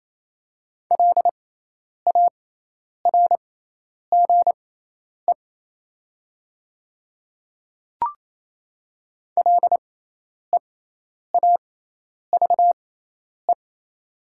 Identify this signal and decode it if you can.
Morse code, heard as a keyed tone, about 28 wpm, spaced as for 10 wpm, 700 Hz. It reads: LARGE LEAVE
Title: 6x Character Spacing - Top 500 Words - 28wpm